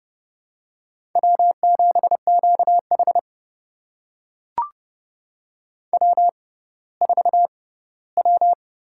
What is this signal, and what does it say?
Morse code, heard as a keyed tone, about 30 wpm, spaced as for 17 wpm, 700 Hz. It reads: W7QH W4W
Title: USA Call Signs - 4 Characters - 6x Character Spacing - 30wpm